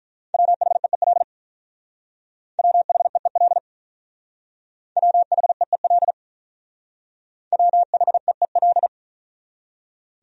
Morse code, heard as a keyed tone, about 44 wpm, 700 Hz. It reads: WHEEL WHEEL WHEEL WHEEL